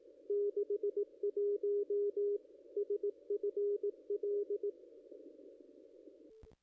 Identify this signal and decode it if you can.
Morse code, heard as a keyed tone, about 18 wpm, 410 Hz. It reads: 61 SFL
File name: contests / ss-cw-k5tr-op-n5rz -0200